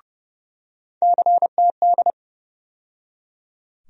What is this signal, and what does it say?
Morse code, heard as a keyed tone, about 30 wpm, 700 Hz. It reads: CTD
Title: USA Call Sign Postfixes - 3 Letters 30wpm